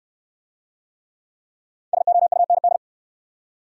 Morse code, heard as a keyed tone, about 65 wpm, 700 Hz. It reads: SOUND